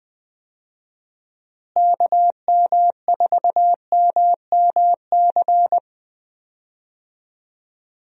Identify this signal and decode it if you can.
Morse code, heard as a keyed tone, about 20 wpm, 700 Hz. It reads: KM4MMC
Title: USA Call Signs - 6 Characters - Repeated 2x - 20wpm